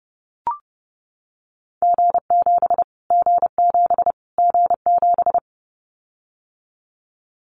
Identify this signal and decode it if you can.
Morse code, heard as a keyed tone, about 30 wpm, 700 Hz. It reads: G7 G7 G7
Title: Callsign Prefixes for Top 25 DXCC - Repeated 3x - 30wpm